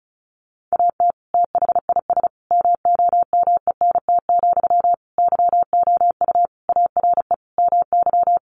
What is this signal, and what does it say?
Morse code, heard as a keyed tone, about 34 wpm, 700 Hz. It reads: AT THIS MOMENT, YOU ARE MY
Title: Sentences from Top 1000 Words 35wpm